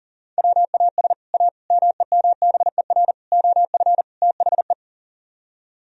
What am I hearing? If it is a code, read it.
Morse code, heard as a keyed tone, about 50 wpm, 700 Hz. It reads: WAS A MEMBER OF THE